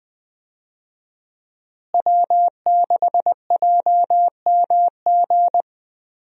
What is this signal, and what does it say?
Morse code, heard as a keyed tone, about 20 wpm, 700 Hz. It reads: W6JMG
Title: USA Call Signs - 5 Characters - Repeated 2x - 20wpm